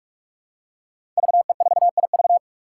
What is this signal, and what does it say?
Morse code, heard as a keyed tone, about 42 wpm, 700 Hz. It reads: VE4IV
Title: Five Challenging Letter-Number Combinations 45wpm